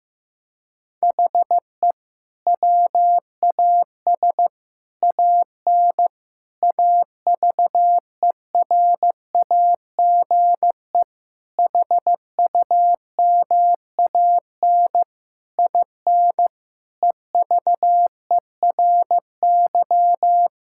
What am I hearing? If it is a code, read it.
Morse code, heard as a keyed tone, about 16 wpm, 700 Hz. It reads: HE WAS AN AVERAGE HUMAN IN EVERY
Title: Sentences from Top 1000 Words - Review Part 6 - 15wpm